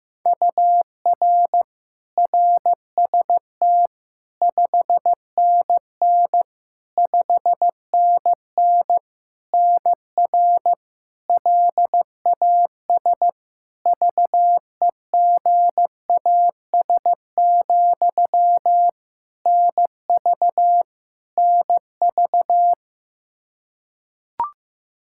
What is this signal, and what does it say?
Morse code, heard as a keyed tone, about 15 wpm, 700 Hz. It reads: UR RST 5NN 5NN NR LAS VEGAS, NV NV
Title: K1USN SST QSO Practice 15wpm